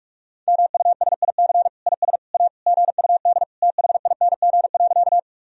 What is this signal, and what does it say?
Morse code, heard as a keyed tone, about 44 wpm, 700 Hz. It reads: MUSIC IS A GUD THING.
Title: Sentences from Top 300 Words and Abbreviations 45wpm